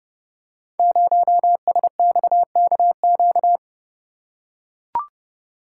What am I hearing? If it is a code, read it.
Morse code, heard as a keyed tone, about 30 wpm, 700 Hz. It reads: 0SXKQ
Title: Code Groups - 5 Letter:Numbers - 30wpm